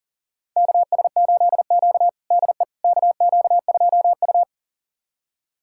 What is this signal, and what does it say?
Morse code, heard as a keyed tone, about 40 wpm, 700 Hz. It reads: KS8Q DE KQ2U
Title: USA Callsign DE Callsign - 4 Characters - 40wpm